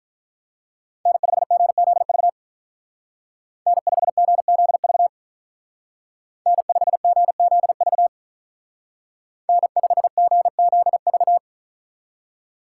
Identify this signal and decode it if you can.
Morse code, heard as a keyed tone, about 44 wpm, 700 Hz. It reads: N5GZV N5GZV N5GZV N5GZV
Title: Speed Racing - USA Call Signs 35wpm